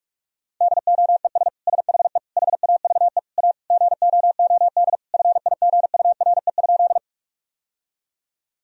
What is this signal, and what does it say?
Morse code, heard as a keyed tone, about 42 wpm, 700 Hz. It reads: DOES SHE HAVE A GOOD FIGURE?
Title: Sentences from Top 1000 Words - Review Part 2 - 45wpm